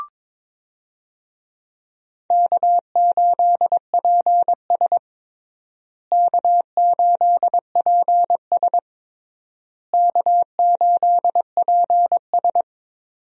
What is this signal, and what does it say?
Morse code, heard as a keyed tone, about 22 wpm, 700 Hz. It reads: K8PS K8PS K8PS
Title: USA Call Signs - 4 Characters - Repeated 3x - 22wpm